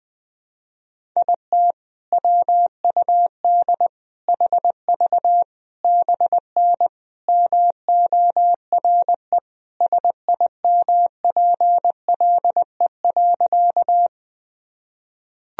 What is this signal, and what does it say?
Morse code, heard as a keyed tone, about 21 wpm, 700 Hz. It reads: IT WUD HV BN MORE SIMPLE.
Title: Sentences from Top 400 Words and Abbreviations 20wpm